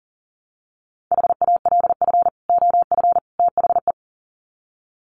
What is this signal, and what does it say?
Morse code, heard as a keyed tone, about 40 wpm, 700 Hz. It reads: HALF OF THE
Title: Sets of 3 Words 40wpm